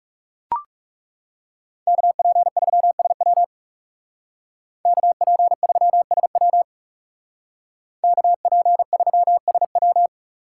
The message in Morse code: KP3SW KP3SW KP3SW

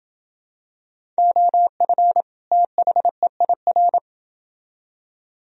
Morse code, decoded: OF THEIR

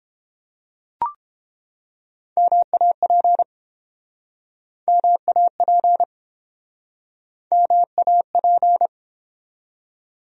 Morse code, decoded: MAP MAP MAP

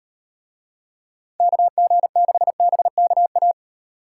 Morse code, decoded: KG6BKA

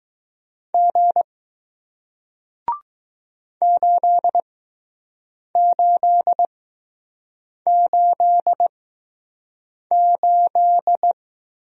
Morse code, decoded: G 8888

